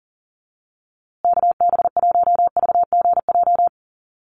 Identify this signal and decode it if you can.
Morse code, heard as a keyed tone, about 40 wpm, 700 Hz. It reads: KB1VGJ
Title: USA Call Signs Repeated 2x 40wpm